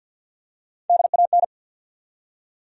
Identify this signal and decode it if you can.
Morse code, heard as a keyed tone, about 50 wpm, 700 Hz. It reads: DAN
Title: USA Common Names - 50wpm